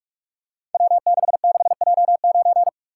Morse code, heard as a keyed tone, about 44 wpm, 700 Hz. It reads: W66J9
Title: Code Groups - 5 Letter:Numbers - 45wpm